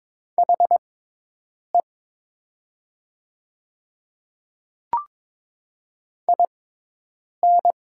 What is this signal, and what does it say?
Morse code, heard as a keyed tone, about 23 wpm, spaced as for 8 wpm, 700 Hz. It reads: HE IN
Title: 6x Character Spacing - Top 100 Words - 22wpm